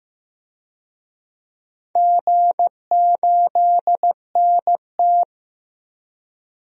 Morse code, heard as a keyed tone, about 15 wpm, 700 Hz. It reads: G8NT
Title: DXCC-100 Call Signs - 4 Characters - 15wpm